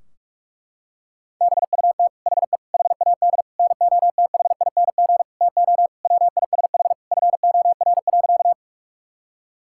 Morse code, heard as a keyed tone, about 46 wpm, 700 Hz. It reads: BUT HE HAD NOTHING TO WISH FOR.
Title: Sentences from Top 1600 Words 45wpm